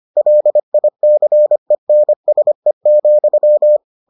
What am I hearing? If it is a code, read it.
Morse code, heard as a keyed tone, about 25 wpm, 600 Hz. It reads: LICENSE,